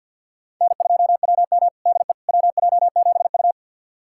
Morse code, heard as a keyed tone, about 50 wpm, 690 Hz. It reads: N2WM DE WJ7U
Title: USA Callsign DE Callsign - 4 Characters - 50wpm